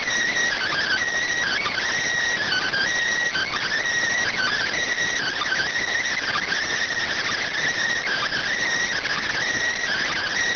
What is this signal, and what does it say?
Emergency Alert System (SAME) tones (header bursts), heard as harsh buzzing bursts. It is heard from 0.5 to 10.3 s.